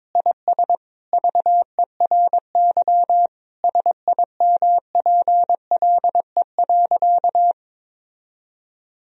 Morse code, heard as a keyed tone, about 22 wpm, 700 Hz. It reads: IS VERY SIMPLE.
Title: Sentences from Top 1000 Words - Review Part 3 - 22wpm